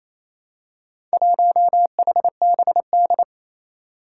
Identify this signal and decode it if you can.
Morse code, heard as a keyed tone, about 28 wpm, 700 Hz. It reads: 1HBD
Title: Four Challenging Letter-Number Combinations 28wpm